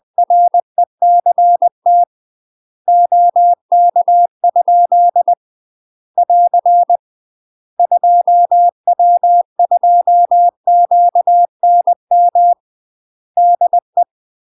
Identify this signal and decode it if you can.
Morse code, heard as a keyed tone, about 20 wpm, 700 Hz. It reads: RECT OK? <AR> 2W2QNM DE
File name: UK MASTER QSOs TRAINING / 20WPM / UK MASTER QSOs 0015